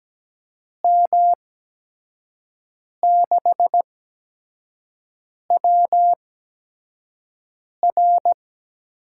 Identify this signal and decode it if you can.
Morse code, heard as a keyed tone, about 17 wpm, spaced as for 5 wpm, 700 Hz. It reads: M6WR
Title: USA Call Signs - 5 Characters - 8x Character Spacing - 17wpm